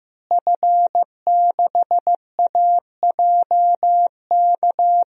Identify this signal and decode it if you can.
Morse code, heard as a keyed tone, about 15 wpm, 700 Hz. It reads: F6AJK
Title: DXCC-100 Call Signs - 5 Characters - Repeated 3x - 15wpm